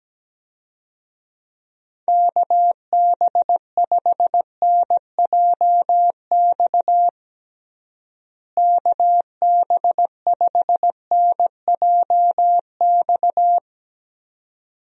Morse code, heard as a keyed tone, about 17 wpm, 700 Hz. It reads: KB5NJX KB5NJX
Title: USA Call Signs - 6 Characters - Repeated 2x - 17wpm